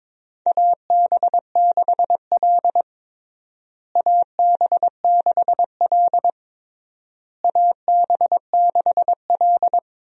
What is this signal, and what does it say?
Morse code, heard as a keyed tone, about 21 wpm, 700 Hz. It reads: AB6L AB6L AB6L
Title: USA Call Signs - 4 Characters - Repeated 3x - 22wpm